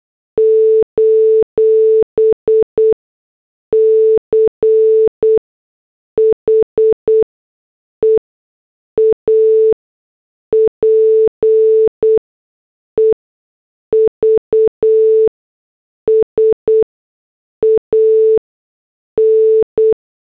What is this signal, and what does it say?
Morse code, heard as a keyed tone, about 8 wpm, spaced as for 6 wpm, 440 Hz. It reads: : CHEAP EVS AN